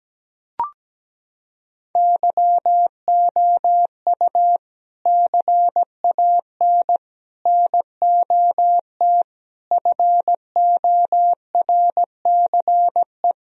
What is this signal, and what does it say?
Morse code, heard as a keyed tone, about 17 wpm, 700 Hz. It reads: YOU CAN NOT FORCE
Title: Sentences from Top 1000 Words - Review Part 3 - 17wpm